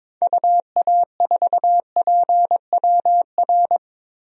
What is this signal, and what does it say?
Morse code, heard as a keyed tone, about 22 wpm, 700 Hz. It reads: UA4PWR
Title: DXCC-100 Call Signs - 6 Characters - 22wpm